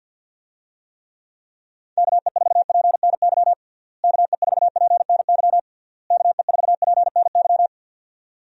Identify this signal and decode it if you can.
Morse code, heard as a keyed tone, about 52 wpm, 700 Hz. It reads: KE4PNY KE4PNY KE4PNY
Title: USA Call Signs - 6 Characters - Repeated 3x - 50wpm